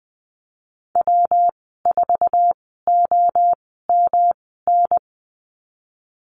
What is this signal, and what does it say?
Morse code, heard as a keyed tone, about 20 wpm, spaced as for 15 wpm, 700 Hz. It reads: W4OMN
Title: 147 Lesson 24g - Letter y - Call Signs 20wpm